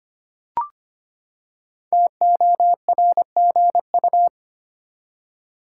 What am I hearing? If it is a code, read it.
Morse code, heard as a keyed tone, about 25 wpm, 700 Hz. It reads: TORGU
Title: Code Groups - 5 Letters - 25wpm